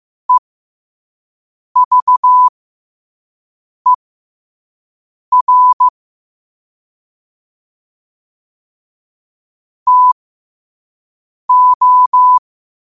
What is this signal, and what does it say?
Morse code, heard as a keyed tone, about 15 wpm, spaced as for 5 wpm, 1000 Hz. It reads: EVER TO
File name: qotdfiles / 5wpm